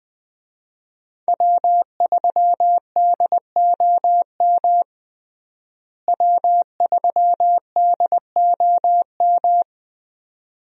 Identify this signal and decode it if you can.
Morse code, heard as a keyed tone, about 20 wpm, 700 Hz. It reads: W3DOM W3DOM
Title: USA Call Signs - 5 Characters - Repeated 2x - 20wpm